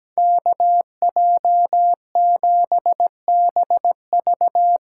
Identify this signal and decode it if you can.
Morse code, heard as a keyed tone, about 18 wpm, 700 Hz. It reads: KJ7BV
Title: Speed Racing - USA 2x2 Call Signs 17wpm